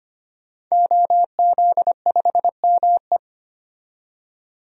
Morse code, heard as a keyed tone, about 25 wpm, 700 Hz. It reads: OZ5ME
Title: DXCC-100 Call Signs - 5 Characters - 25wpm